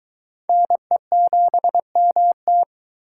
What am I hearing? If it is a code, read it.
Morse code, heard as a keyed tone, about 23 wpm, 700 Hz. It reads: NE7MT